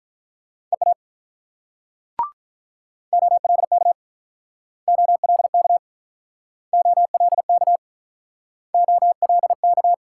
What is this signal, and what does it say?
Morse code, heard as a keyed tone, about 44 wpm, 700 Hz. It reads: EA OLK OLK OLK OLK